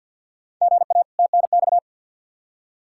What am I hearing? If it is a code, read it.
Morse code, heard as a keyed tone, about 53 wpm, 700 Hz. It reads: GA TNX
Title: K1USN SST QSO Practice 50wpm